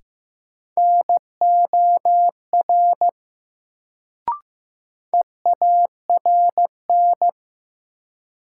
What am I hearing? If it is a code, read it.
Morse code, heard as a keyed tone, about 15 wpm, 700 Hz. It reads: NOR EARN
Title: Sentences from Top 1500 Words - New Word Drill Alt - 15wpm